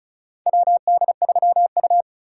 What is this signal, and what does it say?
Morse code, heard as a keyed tone, about 36 wpm, 700 Hz. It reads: WD3U
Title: USA Call Signs - 4 Characters - 3x Character Spacing - 35wpm